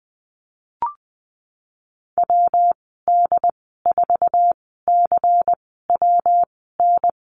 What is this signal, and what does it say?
Morse code, heard as a keyed tone, about 20 wpm, spaced as for 14 wpm, 700 Hz. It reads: WD4CWN
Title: 186 Lesson 30g - Letter b - Call Signs 20wpm